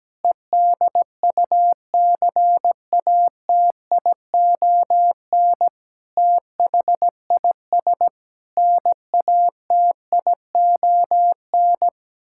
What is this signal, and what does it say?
Morse code, heard as a keyed tone, about 17 wpm, 700 Hz. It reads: EDUCATION THIS NATION
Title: Sentences from Top 1000 Words - Review Part 6 - 17wpm